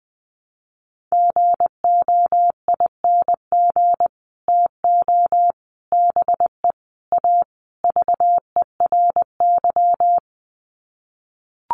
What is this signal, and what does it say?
Morse code, heard as a keyed tone, about 20 wpm, 700 Hz. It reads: GOING TO BE A VERY
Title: Sets of 5 Words Encore 20wpm